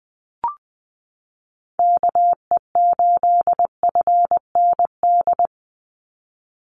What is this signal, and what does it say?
Morse code, heard as a keyed tone, about 20 wpm, 700 Hz. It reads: KE8FND
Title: USA Call Signs 20wpm